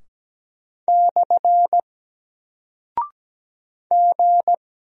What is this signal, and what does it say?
Morse code, heard as a keyed tone, about 17 wpm, 700 Hz. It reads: / G
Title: Single Character (Letter:Number:Punctuation:Prosign) Alt 17wpm